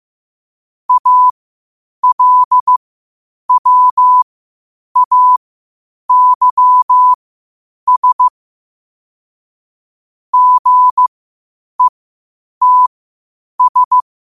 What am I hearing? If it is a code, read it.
Morse code, heard as a keyed tone, about 15 wpm, spaced as for 8 wpm, 1000 Hz. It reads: ALWAYS GETS
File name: qotdfiles / 8wpm